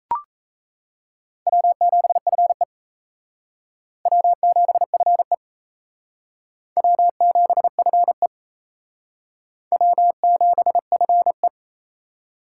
Morse code, heard as a keyed tone, about 34 wpm, 700 Hz. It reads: W7FE W7FE W7FE W7FE
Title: Speed Racing - USA 1x2 Call Signs 28wpm